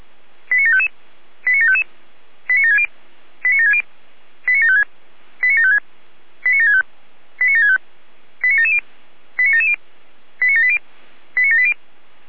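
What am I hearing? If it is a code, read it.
Five-tone selective call (ZVEI), heard as quick runs of five tones, tones of about 80 ms. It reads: 89755 89755 89760 89760 89756 89756 89765 89765 89770 89770 89780 89780
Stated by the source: ZVEI five-tone selective call received on SDR.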